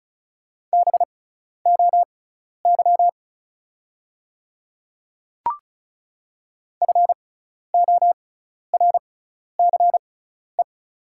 Morse code, heard as a keyed tone, about 34 wpm, spaced as for 12 wpm, 690 Hz. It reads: BOY FORCE
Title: 6x Character Spacing - Top 500 Words - 35wpm